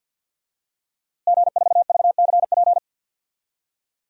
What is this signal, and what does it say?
Morse code, heard as a keyed tone, about 48 wpm, 700 Hz. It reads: G4VCP
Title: DXCC-100 Call Signs - 5 Characters - Repeated 2x - 50wpm